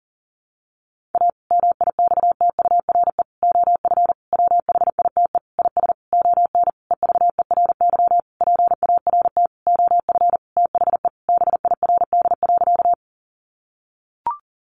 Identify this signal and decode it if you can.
Morse code, heard as a keyed tone, about 40 wpm, 700 Hz. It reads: A MIXTURE OF WHITE IS ON EVERY PART OF THE BIRD.
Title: Sentences from Top 1500 Words 40wpm